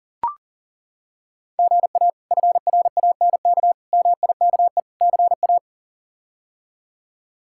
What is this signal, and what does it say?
Morse code, heard as a keyed tone, about 40 wpm, 700 Hz. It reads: GA FRANK MIKE CA
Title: K1USN SST QSO Practice - 40wpm